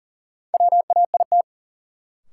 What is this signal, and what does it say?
Morse code, heard as a keyed tone, about 37 wpm, 700 Hz. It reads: WAIT
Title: Sentences from Top 500 Words - New Word Drill - 40wpm